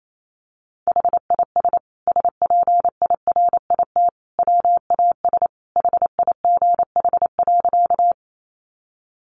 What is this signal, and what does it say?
Morse code, heard as a keyed tone, about 28 wpm, 700 Hz. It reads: HIS SPIRIT WAS HIGH.
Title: Sentences from Top 1400 Words 28wpm